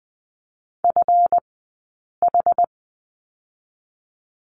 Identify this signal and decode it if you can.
Morse code, heard as a keyed tone, about 20 wpm, spaced as for 8 wpm, 700 Hz. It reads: FH
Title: 171 Lesson 28d - Number 9 - Two Characters 20wpm